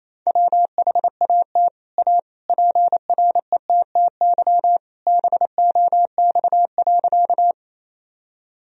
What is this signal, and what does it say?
Morse code, heard as a keyed tone, about 28 wpm, 700 Hz. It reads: WHAT A PRETTY BOX.
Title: Sentences from Top 1000 Words - Review Part 2 - 28wpm